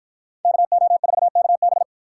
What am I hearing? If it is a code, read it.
Morse code, heard as a keyed tone, about 59 wpm, 700 Hz. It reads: KO4KB